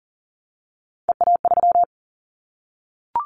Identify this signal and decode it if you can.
Morse code, heard as a keyed tone, about 39 wpm, 700 Hz. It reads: EA3
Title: Callsign Prefixes for Top 25 DXCC - Repeated 3x - 40wpm